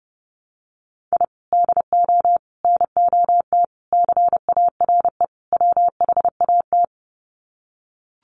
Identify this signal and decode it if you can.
Morse code, heard as a keyed tone, about 30 wpm, 700 Hz. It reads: I DO NOT CARE WHAT
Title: Sets of 5 Words 30wpm